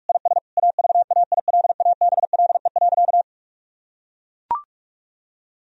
Morse code, heard as a keyed tone, about 44 wpm, 700 Hz. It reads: IS AVAILABLE.
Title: Sentences from Top 1000 Words - Review Part 3 - 45wpm